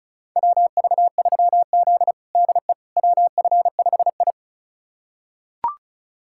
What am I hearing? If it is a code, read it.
Morse code, heard as a keyed tone, about 34 wpm, 700 Hz. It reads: WV3Z DE WF5I
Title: USA Callsign DE Callsign - 4 Characters - 35wpm